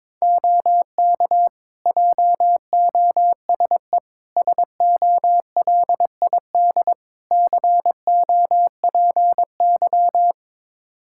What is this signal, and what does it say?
Morse code, heard as a keyed tone, about 22 wpm, 700 Hz. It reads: OK JOSE SOLID COPY
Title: K1USN SST QSO Practice 22wpm